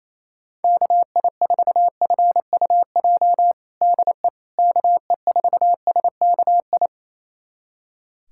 Morse code, heard as a keyed tone, about 27 wpm, 700 Hz. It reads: KI4FUJ DE KE4SKI
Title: USA Callsign DE Callsign - 6 Characters - 28wpm